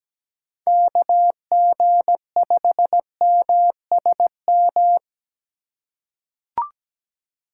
Morse code, heard as a keyed tone, about 17 wpm, 700 Hz. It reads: KG5MSM
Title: USA Call Signs - 6 Characters - 4x Character Spacing - 17wpm